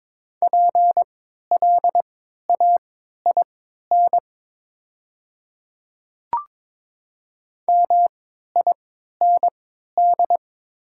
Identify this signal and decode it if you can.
Morse code, heard as a keyed tone, about 22 wpm, spaced as for 12 wpm, 700 Hz. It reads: PLAIN MIND
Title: 3x Character Spacing - Top 500 Words - 22wpm